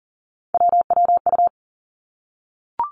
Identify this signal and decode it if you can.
Morse code, heard as a keyed tone, about 38 wpm, 690 Hz. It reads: WWU
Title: Three Challenging Letter-Number Combinations 40wpm